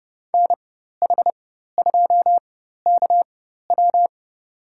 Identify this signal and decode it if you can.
Morse code, heard as a keyed tone, about 29 wpm, spaced as for 14 wpm, 700 Hz. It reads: NH2KW
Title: USA Call Signs - 5 Characters - 4x Character Spacing - 30wpm